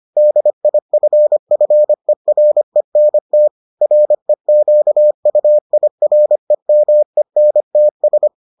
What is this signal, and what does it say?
Morse code, heard as a keyed tone, about 25 wpm, 600 Hz. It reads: DIFFERENT REQUIREMENTS